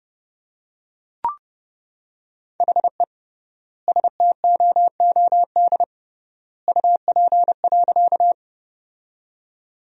Morse code, heard as a keyed tone, about 32 wpm, 700 Hz. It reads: HE STOOD UP.
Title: 3x Word-Spacing - Sentences from Top 500 Words - 30wpm